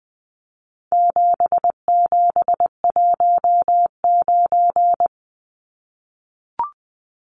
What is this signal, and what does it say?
Morse code, heard as a keyed tone, about 20 wpm, 700 Hz. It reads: 7719
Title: Numbers - 4-Digit 20wpm